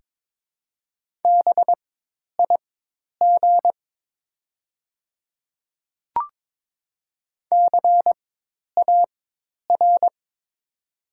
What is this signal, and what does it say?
Morse code, heard as a keyed tone, about 22 wpm, spaced as for 10 wpm, 700 Hz. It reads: BIG CAR
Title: 4x Character Spacing - Top 300 Words - 22wpm